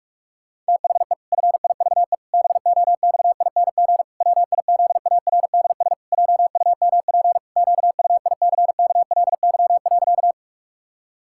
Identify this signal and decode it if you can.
Morse code, heard as a keyed tone, about 46 wpm, 700 Hz. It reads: THE FIVE BOXING WIZARDS JUMP QUICKLY.
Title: Pangram Challenge v2 - 45wpm